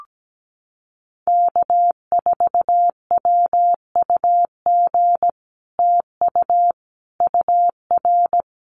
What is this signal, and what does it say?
Morse code, heard as a keyed tone, about 17 wpm, 700 Hz. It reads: K4WUG TU UR
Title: Common QSO Phrases Encore 17wpm